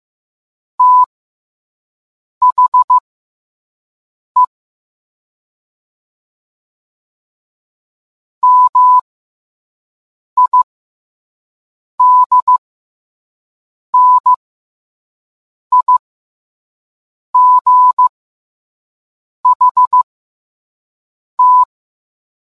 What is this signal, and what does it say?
Morse code, heard as a keyed tone, about 15 wpm, spaced as for 5 wpm, 1000 Hz. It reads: THE MIDNIGHT